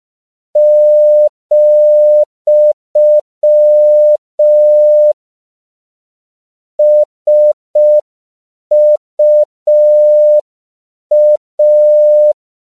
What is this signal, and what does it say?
Morse code, heard as a keyed tone, about 5 wpm, 600 Hz. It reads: , SUA